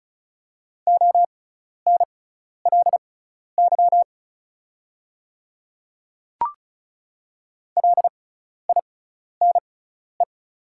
Morse code, heard as a keyed tone, about 36 wpm, spaced as for 12 wpm, 690 Hz. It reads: ONLY LINE
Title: 6x Character Spacing - Top 300 Words - 35wpm